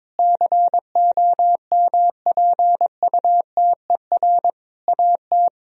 Morse code, heard as a keyed tone, about 22 wpm, 700 Hz. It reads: COMPUTER AT
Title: Sentences from Top 1000 Words - Review Part 3 - 22wpm